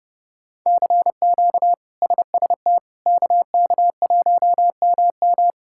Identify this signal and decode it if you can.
Morse code, heard as a keyed tone, about 32 wpm, 700 Hz. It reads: CQ SST KK1MM
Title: K1USN SST QSO Practice - 30wpm